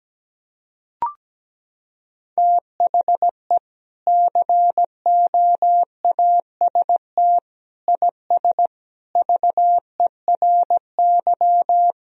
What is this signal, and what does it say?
Morse code, heard as a keyed tone, about 17 wpm, 700 Hz. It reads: THE COAST IS VERY
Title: Sentences from Top 1000 Words - Review Part 4 - 17wpm